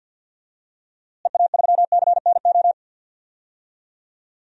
Morse code, heard as a keyed tone, about 47 wpm, 700 Hz. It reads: EA3CNO
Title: DXCC-100 Call Signs - 6 Characters - Repeated 2x - 50wpm